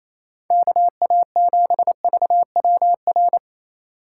Morse code, heard as a keyed tone, about 28 wpm, 700 Hz. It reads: KA7VWR
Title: USA Call Signs - 6 Characters - 4x Character Spacing - 28wpm